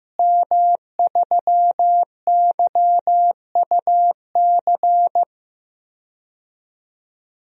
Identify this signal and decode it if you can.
Morse code, heard as a keyed tone, about 15 wpm, 700 Hz. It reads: M3YUC
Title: DXCC-100 Call Signs - 5 Characters - Repeated 3x - 15wpm